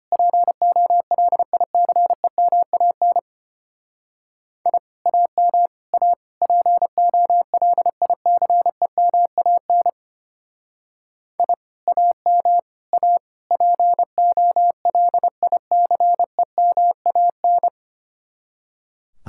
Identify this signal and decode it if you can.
Morse code, heard as a keyed tone, about 30 wpm, 700 Hz. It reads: POLICEMAN I AM A POLICEMAN I AM A POLICEMAN
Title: Speed-Racing Common QSO Phrases - 25wpm